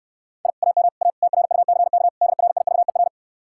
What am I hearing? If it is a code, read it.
Morse code, heard as a keyed tone, about 69 wpm, 700 Hz. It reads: I AM A TRUCK DRIVER